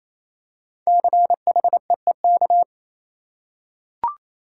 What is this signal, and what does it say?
Morse code, heard as a keyed tone, about 28 wpm, 700 Hz. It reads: CHEEK
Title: Five Letter Words - 28wpm